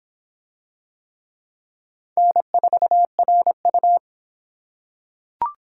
Morse code, heard as a keyed tone, about 22 wpm, 700 Hz. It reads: N4RU